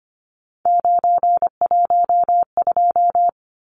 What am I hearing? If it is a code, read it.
Morse code, heard as a keyed tone, about 25 wpm, 700 Hz. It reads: 912
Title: Numbers - 3-Digit 25wpm